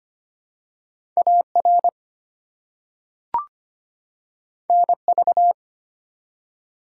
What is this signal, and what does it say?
Morse code, heard as a keyed tone, about 25 wpm, 700 Hz. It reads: AR NV
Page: morse-code-ninja-practice-25wpm